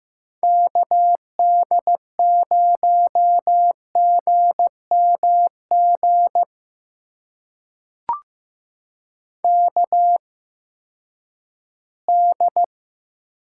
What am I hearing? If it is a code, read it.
Morse code, heard as a keyed tone, about 15 wpm, spaced as for 10 wpm, 700 Hz. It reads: KD0GMG KD
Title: USA Call Signs - 6 Characters - 8x Character Spacing - 15wpm